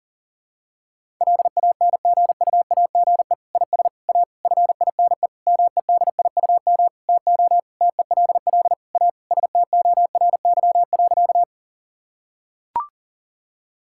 Morse code, heard as a keyed tone, about 40 wpm, 700 Hz. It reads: LANGUAGE IS A FINE MEDIUM TO TELL A STORY.
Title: Sentences from Top 1000 Words - Review Part 6 - 40wpm